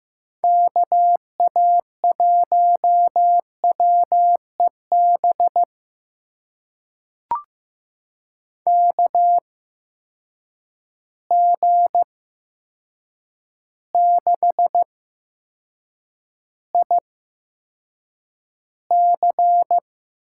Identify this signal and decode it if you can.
Morse code, heard as a keyed tone, about 15 wpm, spaced as for 7 wpm, 700 Hz. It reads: KA1WEB KG6IC
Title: USA Call Signs - 6 Characters - 8x Character Spacing - 15wpm